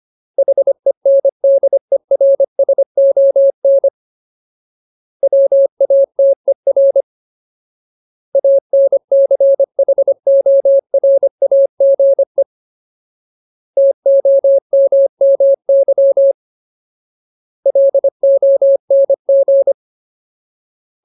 Morse code, heard as a keyed tone, about 25 wpm, 550 Hz. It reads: HENDERSON WATER ANCHORAGE TOMMY LONG
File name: Ultimate Word List / 25WPM Ultimate